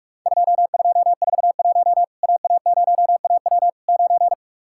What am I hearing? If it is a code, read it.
Morse code, heard as a keyed tone, about 44 wpm, 700 Hz. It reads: 2241 AA0AW 9
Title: Worked All Europe DX Contest - QTC Exchange - 45wpm